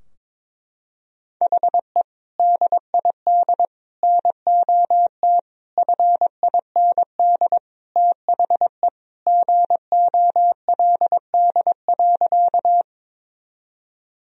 Morse code, heard as a keyed tone, about 22 wpm, 700 Hz. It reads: HE DID NOT FIND THE GOLD.
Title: Sentences from Top 1000 Words - Review Part 3 - 22wpm